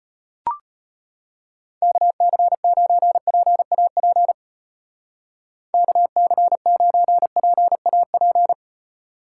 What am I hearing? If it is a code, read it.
Morse code, heard as a keyed tone, about 32 wpm, 700 Hz. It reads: KC9PAP KC9PAP